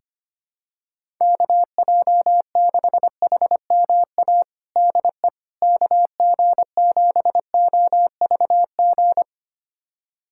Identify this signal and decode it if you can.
Morse code, heard as a keyed tone, about 25 wpm, 700 Hz. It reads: KJ6HMA DE KG7OVG
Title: USA Callsign DE Callsign - 6 Characters - 25wpm